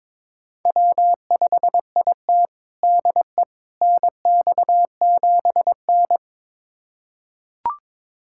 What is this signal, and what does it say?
Morse code, heard as a keyed tone, about 22 wpm, 700 Hz. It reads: W5IT DE NX7N
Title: USA Callsign DE Callsign - 4 Characters - 22wpm